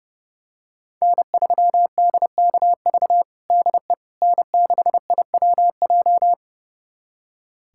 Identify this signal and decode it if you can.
Morse code, heard as a keyed tone, about 32 wpm, 700 Hz. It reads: N3DKV DE N6IWJ
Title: USA Callsign DE Callsign - 5 Characters - 30wpm